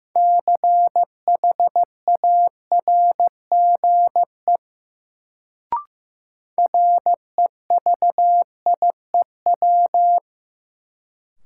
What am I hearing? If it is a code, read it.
Morse code, heard as a keyed tone, about 15 wpm, 700 Hz. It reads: CHARGE REVIEW
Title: Sentences from Top 900 Words - New Word Drill - 15wpm